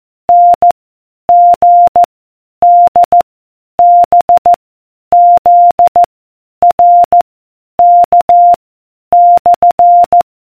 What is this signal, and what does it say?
Morse code, heard as a keyed tone, about 14 wpm, 700 Hz. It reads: N G D B Z R K /